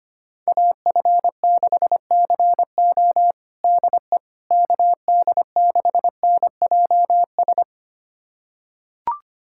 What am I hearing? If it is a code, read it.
Morse code, heard as a keyed tone, about 25 wpm, 700 Hz. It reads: AF6CO DE KD6NJS
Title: USA Callsign DE Callsign 25wpm